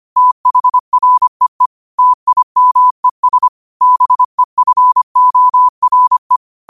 Morse code, heard as a keyed tone, about 25 wpm, 1000 Hz. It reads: THREE TIMES BEFORE
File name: qotdfiles / 25wpm